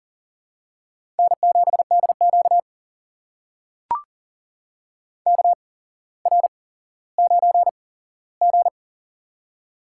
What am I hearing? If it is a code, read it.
Morse code, heard as a keyed tone, about 38 wpm, spaced as for 17 wpm, 700 Hz. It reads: N7DQ KR9G
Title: USA Call Signs - 8x Character Spacing - 40wpm